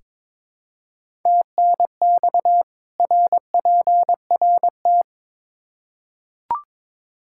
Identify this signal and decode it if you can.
Morse code, heard as a keyed tone, about 22 wpm, 700 Hz. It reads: TNX RPRT
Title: K1USN SST QSO Practice 22wpm